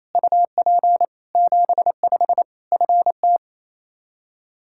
Morse code, heard as a keyed tone, about 28 wpm, 700 Hz. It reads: UP 75 FT
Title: Practice QSO - Exchange 2 - Info - 28wpm